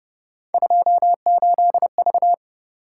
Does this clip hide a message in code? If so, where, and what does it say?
Yes: Morse code from 0.5 to 2.3 s, heard as a keyed tone, about 29 wpm, 700 Hz. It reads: 28V